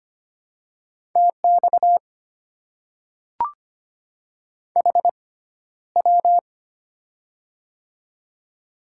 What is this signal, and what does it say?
Morse code, heard as a keyed tone, about 25 wpm, spaced as for 13 wpm, 700 Hz. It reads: TX HW
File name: Top 90 QSO Elements - 6x Character Spacing - 25wpm